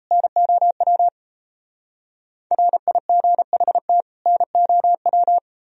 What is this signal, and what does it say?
Morse code, heard as a keyed tone, about 35 wpm, 700 Hz. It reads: NOW RIGHT NOW
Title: Speed-Racing - Sets of 2 Words 28wpm